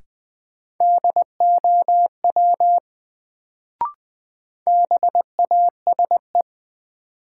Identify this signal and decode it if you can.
Morse code, heard as a keyed tone, about 20 wpm, 700 Hz. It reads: DOW BASE
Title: Sets of 5 Words - New Word Drill - 20wpm